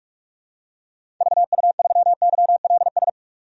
Morse code, heard as a keyed tone, about 46 wpm, 700 Hz. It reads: VU3YLS
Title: DXCC-100 Call Signs - 6 Characters - 45wpm